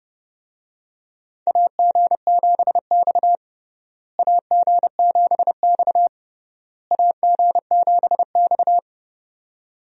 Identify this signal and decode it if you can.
Morse code, heard as a keyed tone, about 30 wpm, 700 Hz. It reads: AG7X AG7X AG7X
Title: USA Call Signs - 4 Characters - Repeated 3x - 30wpm